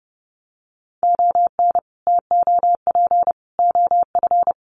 Morse code, heard as a keyed tone, about 30 wpm, 700 Hz. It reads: ON TOP OF
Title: Sets of 3 Words 30wpm